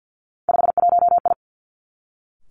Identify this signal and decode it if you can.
Morse code, heard as a keyed tone, about 50 wpm, 700 Hz. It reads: 51I
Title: Three Challenging Letter-Number Combinations 50wpm